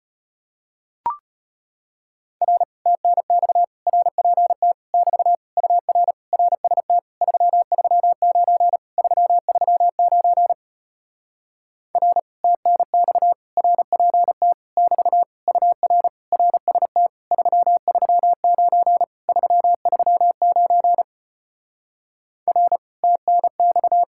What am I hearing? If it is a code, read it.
Morse code, heard as a keyed tone, about 33 wpm, 700 Hz. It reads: R TNX RPT <BT> UR RST 339 339 R TNX RPT <BT> UR RST 339 339 R TNX